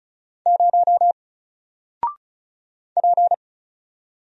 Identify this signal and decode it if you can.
Morse code, heard as a keyed tone, about 36 wpm, 700 Hz. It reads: 0 P
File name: Single-Letter-Number-35wpm